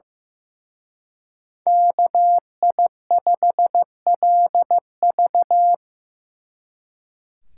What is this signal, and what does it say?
Morse code, heard as a keyed tone, about 18 wpm, 700 Hz. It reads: KI5LV
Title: Speed Racing - USA 2x2 Call Signs 15wpm